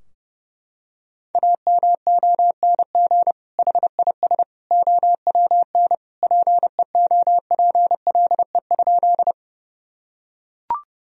Morse code, heard as a keyed tone, about 30 wpm, 700 Hz. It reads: AMONG HIS OWN PEOPLE?
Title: Sentences from Top 1000 Words - Review Part 3 - 30wpm